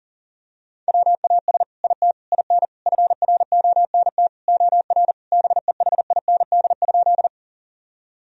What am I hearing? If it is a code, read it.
Morse code, heard as a keyed tone, about 40 wpm, 700 Hz. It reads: WAS IT IN FRONT OR BEHIND?
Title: Sentences from Top 1000 Words - Review Part 2 - 40wpm